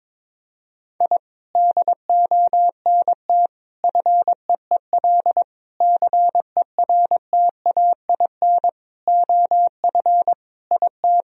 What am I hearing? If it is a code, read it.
Morse code, heard as a keyed tone, about 22 wpm, 700 Hz. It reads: I DONT FEEL CERTAIN OF IT
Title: Sentences from Top 1000 Words - Review Part 2 - 22wpm